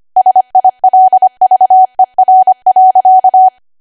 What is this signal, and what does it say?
Morse code, heard as a keyed tone, about 25 wpm, 750 Hz. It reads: SILVER.
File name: Archive / 25 WPM / 25WPM